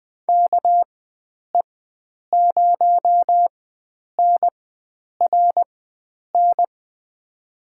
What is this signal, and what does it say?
Morse code, heard as a keyed tone, about 20 wpm, spaced as for 9 wpm, 700 Hz. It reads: KE0NRN
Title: USA Call Signs - 4x Character Spacing - 20wpm